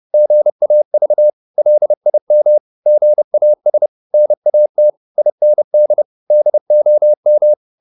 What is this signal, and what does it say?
Morse code, heard as a keyed tone, about 30 wpm, 600 Hz. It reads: GAV LIM GAS NAT IND DOM
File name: Korte ord DK 30wpm